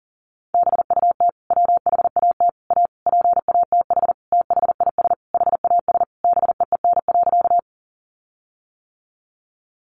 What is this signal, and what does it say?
Morse code, heard as a keyed tone, about 40 wpm, 700 Hz. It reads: BUT WHAT A PATH THIS HAS BEEN.
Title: Sentences from Top 1400 Words 40wpm